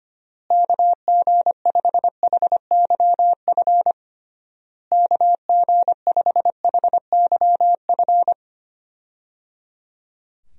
Morse code, heard as a keyed tone, about 25 wpm, 700 Hz. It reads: KG5HYF KG5HYF
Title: USA Call Signs - 6 Characters - Repeated 3x - 25wpm